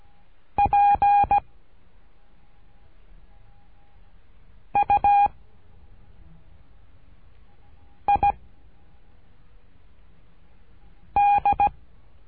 Morse code, heard as a keyed tone, about 17 wpm, 810 Hz. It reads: P U I D
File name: cursos / CURSO DE TELEGRAFIA / Telegrafia - Trilha 13